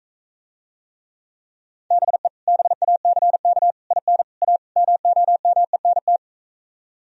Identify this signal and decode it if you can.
Morse code, heard as a keyed tone, about 45 wpm, 700 Hz. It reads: BE BACK IN A MOMENT